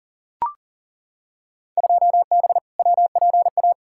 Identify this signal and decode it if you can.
Morse code, heard as a keyed tone, about 40 wpm, 700 Hz. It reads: 2B WPA
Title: Field Day Exchanges Encore - 40wpm